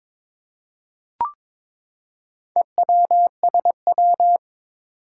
Morse code, heard as a keyed tone, about 22 wpm, 700 Hz. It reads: EWSW
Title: Four Challenging Letter-Number Combinations 22wpm